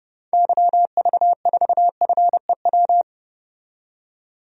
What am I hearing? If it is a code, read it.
Morse code, heard as a keyed tone, about 31 wpm, 700 Hz. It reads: YV4FEW
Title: DXCC-100 Call Signs - 6 Characters - 30wpm